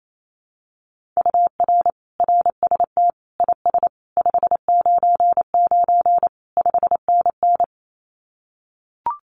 Morse code, heard as a keyed tone, about 28 wpm, 700 Hz. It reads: UR RST IS 599 5NN
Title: Common QSO Phrases Encore 28wpm